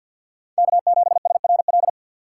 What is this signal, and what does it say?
Morse code, heard as a keyed tone, about 48 wpm, 700 Hz. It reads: K7SRL
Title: USA Call Signs - 5 Characters - 6x Character Spacing - 50wpm